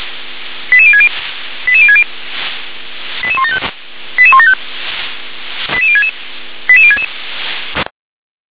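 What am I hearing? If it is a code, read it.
Five-tone selective call (ZVEI), heard as quick runs of five tones, tones of about 70 ms. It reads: 88060 88060 88175 88175 88066 88066
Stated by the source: ZVEI five-tone call received on SDR.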